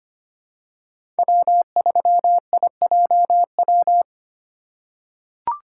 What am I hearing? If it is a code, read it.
Morse code, heard as a keyed tone, about 25 wpm, 700 Hz. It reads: W3IJW
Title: USA Call Signs - 5 Characters - 4x Character Spacing - 25wpm